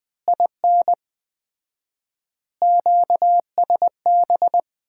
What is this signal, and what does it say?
Morse code, heard as a keyed tone, about 20 wpm, 700 Hz. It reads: IN QSB
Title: Common QSO Phrases - 4x Word-Spacing - 20wpm